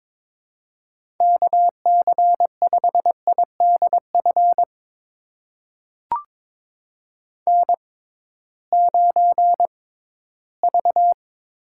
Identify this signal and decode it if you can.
Morse code, heard as a keyed tone, about 22 wpm, spaced as for 14 wpm, 700 Hz. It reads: KC5IDF N9V
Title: USA Call Signs - 6x Character Spacing - 22wpm